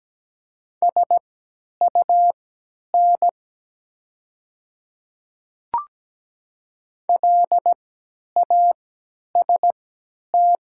Morse code, heard as a keyed tone, about 17 wpm, spaced as for 10 wpm, 690 Hz. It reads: SUN LAST
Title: 3x Character Spacing - Top 300 Words - 17wpm